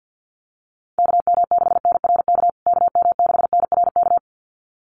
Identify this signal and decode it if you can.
Morse code, heard as a keyed tone, about 50 wpm, 700 Hz. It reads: KM6NRK KM6NRK
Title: USA Call Signs Repeated 2x 50wpm